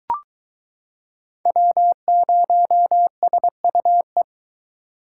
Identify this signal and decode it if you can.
Morse code, heard as a keyed tone, about 23 wpm, 700 Hz. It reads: W0SUE